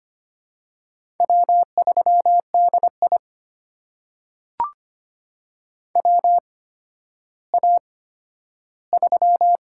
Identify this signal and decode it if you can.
Morse code, heard as a keyed tone, about 25 wpm, spaced as for 12 wpm, 700 Hz. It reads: W3DI WA3
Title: USA Call Signs - 8x Character Spacing - 25wpm